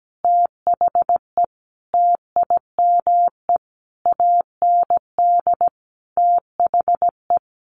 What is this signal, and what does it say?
Morse code, heard as a keyed tone, about 17 wpm, 700 Hz. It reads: THE TIME AND THE
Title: Sentences from Top 1500 Words 17wpm